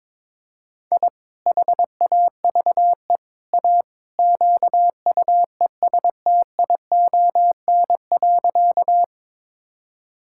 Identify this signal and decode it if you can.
Morse code, heard as a keyed tone, about 22 wpm, 700 Hz. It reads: I HAVE A QUESTION.
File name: practice / 22wpm / Sentences with 3-Word Phrases - 22wpm